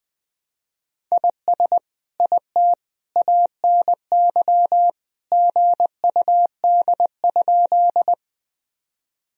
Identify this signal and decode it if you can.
Morse code, heard as a keyed tone, about 20 wpm, 700 Hz. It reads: IS IT ANY GUD?
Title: Sentences from Top 200 Words and Abbreviations 20wpm